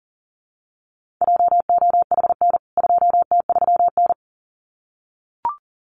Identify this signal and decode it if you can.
Morse code, heard as a keyed tone, about 40 wpm, 700 Hz. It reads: JOHN 2T3N
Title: CWT Exchange 40wpm